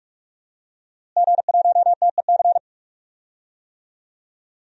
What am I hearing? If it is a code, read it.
Morse code, heard as a keyed tone, about 46 wpm, 700 Hz. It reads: G1TEC
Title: DXCC-100 Call Signs - 5 Characters - Repeated 3x - 45wpm